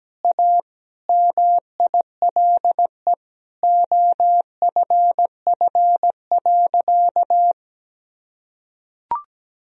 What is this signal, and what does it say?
Morse code, heard as a keyed tone, about 17 wpm, 700 Hz. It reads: A MILE OFF.
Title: Sentences from Top 1000 Words - Review Part 1 - 17wpm